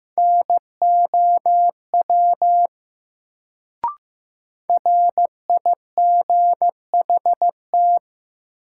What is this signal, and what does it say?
Morse code, heard as a keyed tone, about 15 wpm, 700 Hz. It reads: NOW RIGHT
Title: Sets of 2 Words - New Word Drill - 15wpm